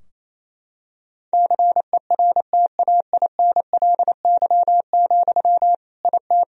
Morse code, heard as a keyed tone, about 28 wpm, 700 Hz. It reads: CERTAINLY, IT
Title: Sentences from Top 1000 Words - Review Part 2 - 28wpm